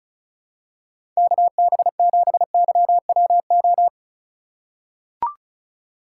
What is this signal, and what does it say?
Morse code, heard as a keyed tone, about 36 wpm, 700 Hz. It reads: KB7YWO
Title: USA Call Signs - 6 Characters - 3x Character Spacing - 35wpm